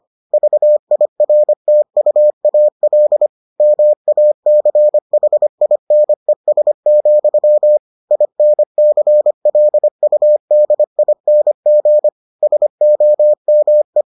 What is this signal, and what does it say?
Morse code, heard as a keyed tone, about 25 wpm, 600 Hz. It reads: VIRTUAL MACHINES, INCLUDING SOME